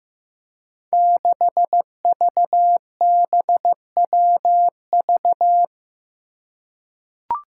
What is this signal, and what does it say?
Morse code, heard as a keyed tone, about 15 wpm, 700 Hz. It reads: 6VBWV
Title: Five Challenging Letter-Number Combinations 15wpm